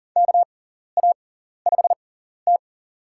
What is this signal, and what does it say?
Morse code, heard as a keyed tone, about 40 wpm, spaced as for 14 wpm, 700 Hz. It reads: KA5T
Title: USA Call Signs - 4 Characters - 6x Character Spacing - 40wpm